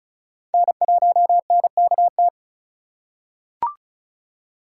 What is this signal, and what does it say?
Morse code, heard as a keyed tone, about 34 wpm, 700 Hz. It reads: N1NKT
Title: USA Call Signs - 3x Character Spacing - 35wpm